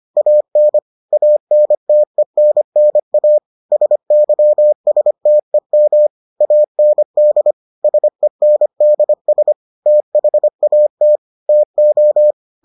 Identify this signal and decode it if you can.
Morse code, heard as a keyed tone, about 25 wpm, 600 Hz. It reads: AN ANTENNA SYSTEM AND SENDS THAT TO